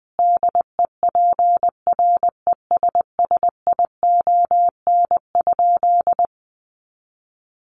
Morse code, heard as a keyed tone, about 20 wpm, 700 Hz. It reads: DEPRESSION?
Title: Sentences from Top 1300 Words 20wpm